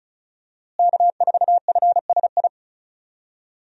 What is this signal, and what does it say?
Morse code, heard as a keyed tone, about 36 wpm, 700 Hz. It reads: K4FSI
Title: USA Call Signs - 5 Characters - Repeated 2x - 35wpm